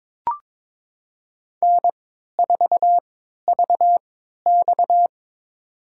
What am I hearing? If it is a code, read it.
Morse code, heard as a keyed tone, about 22 wpm, spaced as for 13 wpm, 700 Hz. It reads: N4VX
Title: USA Call Signs - 4 Characters - 3x Character Spacing - 22wpm